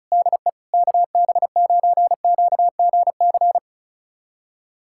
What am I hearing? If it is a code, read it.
Morse code, heard as a keyed tone, about 36 wpm, 700 Hz. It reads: DE KB9QGC
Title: USA Callsign DE Callsign 35wpm